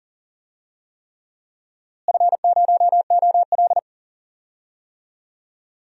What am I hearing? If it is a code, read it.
Morse code, heard as a keyed tone, about 40 wpm, 700 Hz. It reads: F0OL
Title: DXCC-100 Call Signs - 4 Characters - Repeated 3x - 40wpm